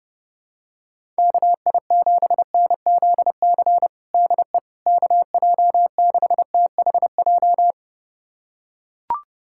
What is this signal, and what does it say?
Morse code, heard as a keyed tone, about 30 wpm, 700 Hz. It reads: KI7NZC DE KJ6THJ
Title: USA Callsign DE Callsign - 6 Characters - 30wpm